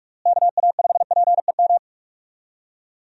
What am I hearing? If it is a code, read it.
Morse code, heard as a keyed tone, about 43 wpm, 700 Hz. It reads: KA5PEM
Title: USA Call Signs - 6 Characters - Repeated 2x - 45wpm